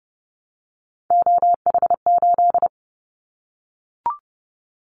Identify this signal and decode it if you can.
Morse code, heard as a keyed tone, about 30 wpm, 700 Hz. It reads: OH8
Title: Callsign Prefixes for Top 25 DXCC - 30wpm